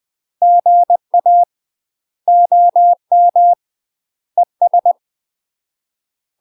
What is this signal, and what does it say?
Morse code, heard as a keyed tone, about 20 wpm, 700 Hz. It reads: GA OM ES
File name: UK MASTER QSOs TRAINING / 20WPM / UK MASTER QSOs 0032